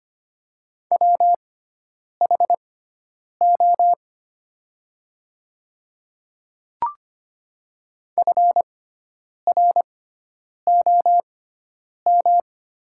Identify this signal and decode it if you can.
Morse code, heard as a keyed tone, about 25 wpm, spaced as for 9 wpm, 700 Hz. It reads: WHO FROM
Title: 6x Character Spacing - Top 100 Words - 25wpm